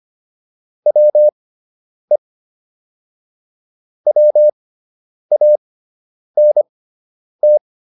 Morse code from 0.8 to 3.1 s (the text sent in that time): WE